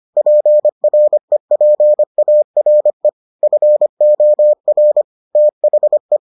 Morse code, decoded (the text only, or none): PREPARE FOR THE